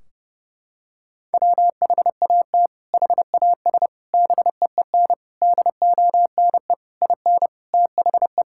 Morse code, heard as a keyed tone, about 30 wpm, 700 Hz. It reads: WHAT HAS BEEN DONE IN THE
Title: Sentences from Top 1600 Words 30wpm